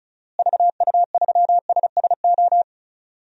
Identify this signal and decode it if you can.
Morse code, heard as a keyed tone, about 36 wpm, 700 Hz. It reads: VU3SSO
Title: DXCC-100 Call Signs - 6 Characters - Repeated 2x - 35wpm